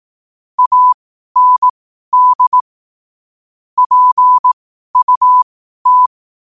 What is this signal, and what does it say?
Morse code, heard as a keyed tone, about 18 wpm, spaced as for 12 wpm, 1000 Hz. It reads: AND PUT
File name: qotdfiles / 12wpm